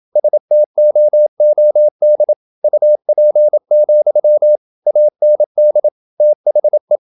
Morse code, heard as a keyed tone, about 27 wpm, 600 Hz. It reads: STOOD UP, AND THE
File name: Never 0027wpm